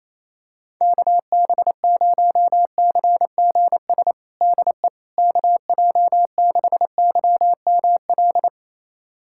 Morse code, heard as a keyed tone, about 28 wpm, 700 Hz. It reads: KB0CGS DE KJ6YML
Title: USA Callsign DE Callsign - 6 Characters - 28wpm